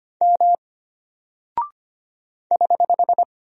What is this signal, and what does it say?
Morse code, heard as a keyed tone, about 25 wpm, 700 Hz. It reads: M <HH>
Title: Single Character (Letter:Number:Punctuation:Prosign) Alt 25wpm